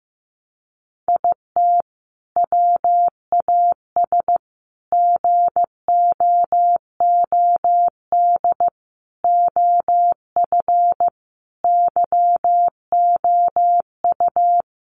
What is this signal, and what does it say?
Morse code, heard as a keyed tone, about 15 wpm, 700 Hz. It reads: IT WAS GOOD OF YOU
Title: Sentences from Top 1100 Words 15wpm